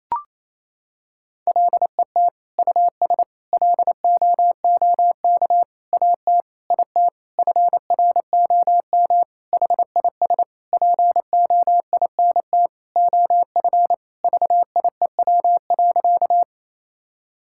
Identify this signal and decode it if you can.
Morse code, heard as a keyed tone, about 28 wpm, 700 Hz. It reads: LET US LOOK AT IT FROM HIS POINT OF VIEW.
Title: Sentences from Top 1000 Words - Review Part 6 - 28wpm